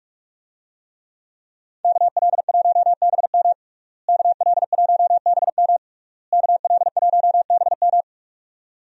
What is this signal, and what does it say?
Morse code, heard as a keyed tone, about 42 wpm, 700 Hz. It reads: KL1BM KL1BM KL1BM
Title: USA Call Signs - 5 Characters - Repeated 3x - 45wpm